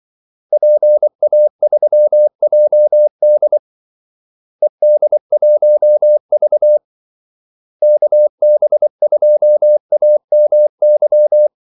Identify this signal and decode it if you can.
Morse code, heard as a keyed tone, about 24 wpm, 600 Hz. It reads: PA3JD ED1V KB2AMY
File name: Callsigns-24wpm